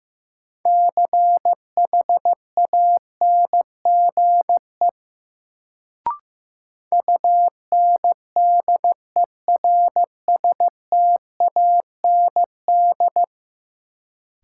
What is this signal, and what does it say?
Morse code, heard as a keyed tone, about 15 wpm, 700 Hz. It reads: CHANGE UNDERSTAND
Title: Sentences from Top 600 Words - New Word Drill - 15wpm